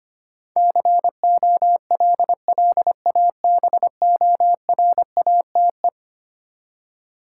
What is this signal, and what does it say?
Morse code, heard as a keyed tone, about 25 wpm, 700 Hz. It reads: COLLABORATE
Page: morse-code-ninja-practice-25wpm